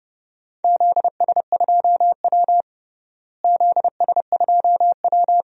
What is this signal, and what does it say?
Morse code, heard as a keyed tone, about 32 wpm, 700 Hz. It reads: ZS2W ZS2W
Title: DXCC-100 Call Signs - 4 Characters - Repeated 3x - 30wpm